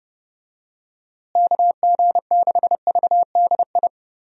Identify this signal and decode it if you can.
Morse code, heard as a keyed tone, about 30 wpm, 700 Hz. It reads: KG6VDI
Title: USA Call Signs - 6 Characters - Repeated 3x - 30wpm